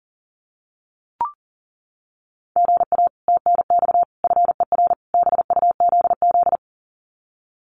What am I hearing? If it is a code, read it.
Morse code, heard as a keyed tone, about 40 wpm, 700 Hz. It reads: GA TNX FER BUZZ
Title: Common QSO Phrases Encore 40wpm